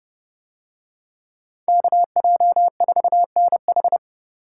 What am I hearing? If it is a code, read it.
Morse code, heard as a keyed tone, about 30 wpm, 700 Hz. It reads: KJ4NH